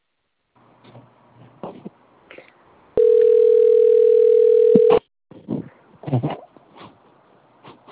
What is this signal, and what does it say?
Ringback tone (North America), 440 + 480 Hz.